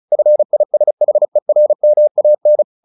Morse code, heard as a keyed tone, about 34 wpm, 600 Hz. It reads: FISHERMAN
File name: the old man and the sea 2 35wpm